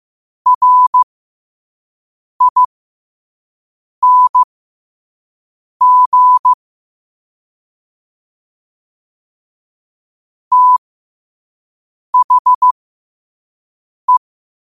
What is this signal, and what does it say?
Morse code, heard as a keyed tone, about 15 wpm, spaced as for 5 wpm, 1000 Hz. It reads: RING THE
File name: qotdfiles / 5wpm